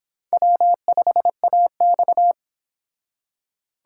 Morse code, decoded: W5AX

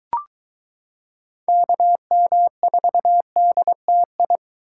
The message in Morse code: KM4DTI

